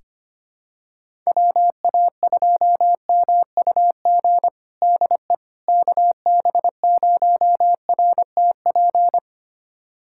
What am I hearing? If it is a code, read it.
Morse code, heard as a keyed tone, about 25 wpm, 700 Hz. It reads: WA2MUG DE KB0RTP
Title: USA Callsign DE Callsign - 6 Characters - 25wpm